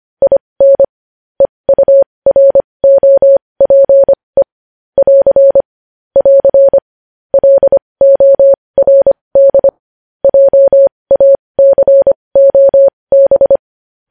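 Morse code, heard as a keyed tone, about 25 wpm, spaced as for 20 wpm, 550 Hz. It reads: IN EUROPE <AR> <AR> LORD JACOB
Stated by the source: Morse code news headlines sent at 20 wpm.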